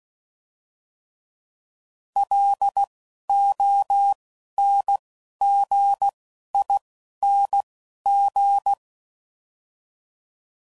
Morse code, heard as a keyed tone, about 16 wpm, 800 Hz. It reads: L O N G I N G
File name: Suffixes / Special INGs / Single INGs